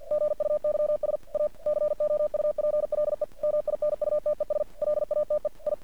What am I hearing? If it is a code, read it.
Morse code, heard as a keyed tone, about 50 wpm, 600 Hz. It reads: GUYS A COUPLE MINUTES LATE I
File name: archived / CRSnet